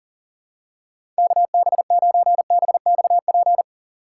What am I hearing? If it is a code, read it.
Morse code, heard as a keyed tone, about 37 wpm, 700 Hz. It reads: KB9BXP
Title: USA Call Signs - 6 Characters - 4x Character Spacing - 40wpm